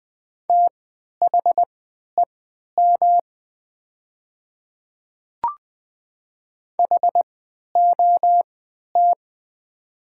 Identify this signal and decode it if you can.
Morse code, heard as a keyed tone, about 20 wpm, spaced as for 11 wpm, 700 Hz. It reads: THEM HOT
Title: 3x Character Spacing - Top 300 Words - 20wpm